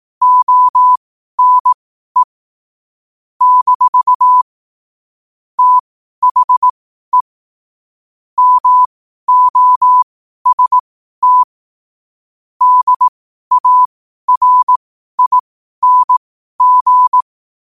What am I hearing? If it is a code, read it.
Morse code, heard as a keyed tone, about 18 wpm, spaced as for 12 wpm, 1000 Hz. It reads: ONE - THE MOST DARING
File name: qotdfiles / 12wpm